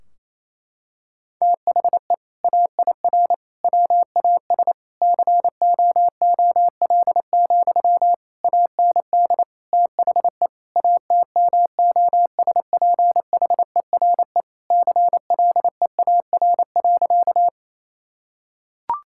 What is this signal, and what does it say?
Morse code, heard as a keyed tone, about 28 wpm, 700 Hz. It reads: THE AIR WAS COOL, AND THE ATMOSPHERE CLEAR.
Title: Sentences from Top 1600 Words 28wpm